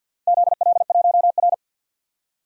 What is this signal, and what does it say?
Morse code, heard as a keyed tone, about 52 wpm, 700 Hz. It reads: ZL1R
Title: DXCC-100 Call Signs - 4 Characters - 50wpm